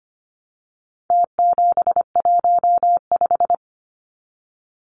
Morse code, heard as a keyed tone, about 25 wpm, 700 Hz. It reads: T715
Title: Numbers - Contest Style - 4-Digit 25wpm